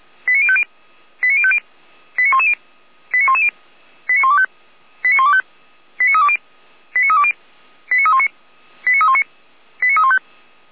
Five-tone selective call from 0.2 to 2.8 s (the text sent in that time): 89050 89050 89110